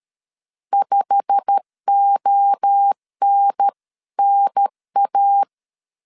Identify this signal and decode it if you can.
Morse code, heard as a keyed tone, about 13 wpm, 800 Hz. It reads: 5ONNA
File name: CW / DET / 1 9 1500C S A 12wpm 3800